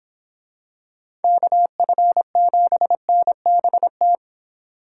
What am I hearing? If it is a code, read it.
Morse code, heard as a keyed tone, about 25 wpm, 700 Hz. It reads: KF7NBT